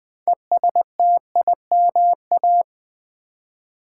Morse code, heard as a keyed tone, about 20 wpm, 700 Hz. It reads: ESTIMA
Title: Build A Word - 2 Letters At A Time - 8-Letter Words - 20wpm